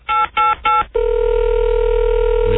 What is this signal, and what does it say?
DTMF, heard as telephone touch tones, tones of about 180 ms. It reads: 888